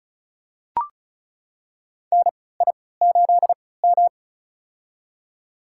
Morse code, heard as a keyed tone, about 34 wpm, spaced as for 20 wpm, 700 Hz. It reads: NI8M
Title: USA Call Signs - 4 Characters - 3x Character Spacing - 35wpm